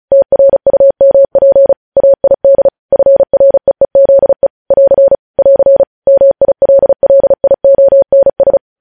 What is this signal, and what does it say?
Morse code, heard as a keyed tone, about 36 wpm, 550 Hz. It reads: TRUMP AID FREEZE <AR> <AR> MILLIONS
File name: podcast / News headlines-1x-35x35